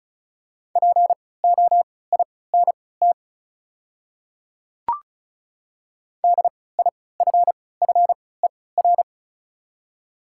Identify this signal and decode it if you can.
Morse code, heard as a keyed tone, about 36 wpm, spaced as for 20 wpm, 700 Hz. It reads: POINT DIFFER
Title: 3x Character Spacing - Top 200 Words - 35wpm